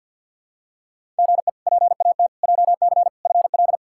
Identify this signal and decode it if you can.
Morse code, heard as a keyed tone, about 47 wpm, 690 Hz. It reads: GE PAT JC FL
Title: K1USN SST QSO Practice - 50wpm